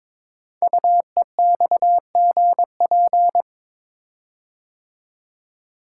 Morse code, heard as a keyed tone, about 22 wpm, 690 Hz. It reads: UEXGP
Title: Code Groups - 5 Letters - 22wpm